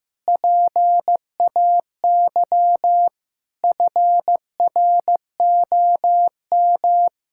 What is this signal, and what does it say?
Morse code, heard as a keyed tone, about 15 wpm, 700 Hz. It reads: PAY FROM
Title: Pangram Challenge v2 - 15wpm